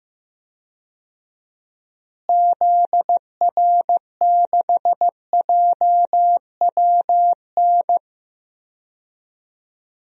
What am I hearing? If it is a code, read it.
Morse code, heard as a keyed tone, about 15 wpm, 700 Hz. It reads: ZR6JWN
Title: DXCC-100 Call Signs - 6 Characters - Repeated 2x - 15wpm